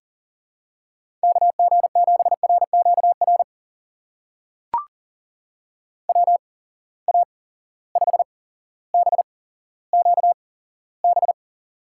Morse code, heard as a keyed tone, about 38 wpm, spaced as for 17 wpm, 700 Hz. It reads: KG7RQR WA5BQB
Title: USA Call Signs - 6 Characters - 8x Character Spacing - 40wpm